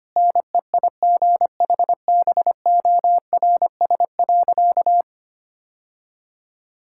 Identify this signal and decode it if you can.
Morse code, heard as a keyed tone, about 25 wpm, 700 Hz. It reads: NEIGHBORS.
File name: practice / 25wpm / Sentences with 3-Word Phrases - 25wpm